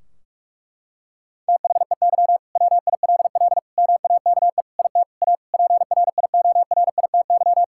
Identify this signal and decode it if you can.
Morse code, heard as a keyed tone, about 41 wpm, 700 Hz. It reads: THEY WILL MAKE IT A PRIORITY
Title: Sentences from Top 1600 Words 45wpm